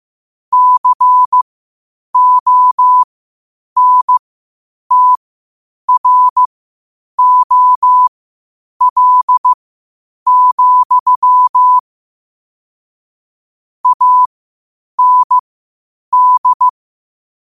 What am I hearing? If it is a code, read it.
Morse code, heard as a keyed tone, about 15 wpm, spaced as for 8 wpm, 1000 Hz. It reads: CONTROL, AND
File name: qotdfiles / 8wpm